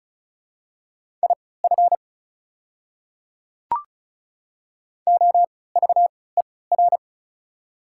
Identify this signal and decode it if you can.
Morse code, heard as a keyed tone, about 36 wpm, spaced as for 20 wpm, 690 Hz. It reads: IF OVER
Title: 3x Character Spacing - Top 200 Words - 35wpm